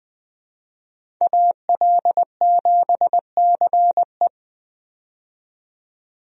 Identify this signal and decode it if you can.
Morse code, heard as a keyed tone, about 20 wpm, 700 Hz. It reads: AL7CE
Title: USA Call Signs - 5 Characters - 20wpm